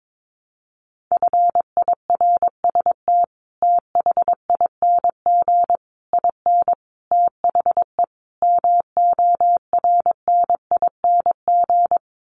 Morse code, heard as a keyed tone, about 22 wpm, 700 Hz. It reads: FIRST THING IN THE MORNING
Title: Sets of 5 Words Encore2 22wpm